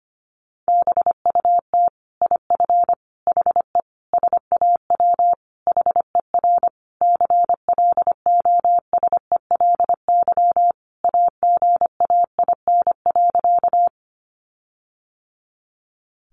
Morse code, heard as a keyed tone, about 25 wpm, 700 Hz. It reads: BUT IF HE SAW HER CLOSELY AGAIN.
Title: Sentences from Top 1400 Words 25wpm